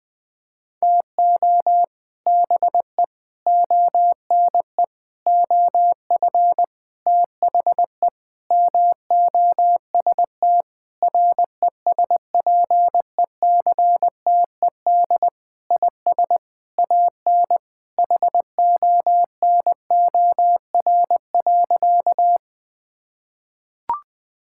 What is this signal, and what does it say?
Morse code, heard as a keyed tone, about 19 wpm, 700 Hz. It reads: TO BE ONE OF THE MOST RESPECTED IS AN HONOR.
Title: Sentences with 6-Word Phrases - 20wpm